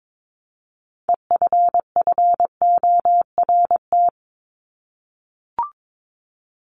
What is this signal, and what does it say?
Morse code, heard as a keyed tone, about 22 wpm, 700 Hz. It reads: EFFORT
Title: Top 1001-1250 Words 22wpm